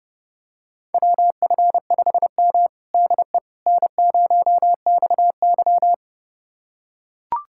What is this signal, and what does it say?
Morse code, heard as a keyed tone, about 29 wpm, 700 Hz. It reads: WF5M DE N0XY
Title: USA Callsign DE Callsign - 4 Characters - 30wpm